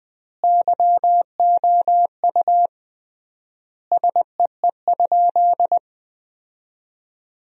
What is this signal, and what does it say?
Morse code, heard as a keyed tone, about 20 wpm, 700 Hz. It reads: YOU SEE?
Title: 3x Word-Spacing - Sentences from Top 100 Words - 20wpm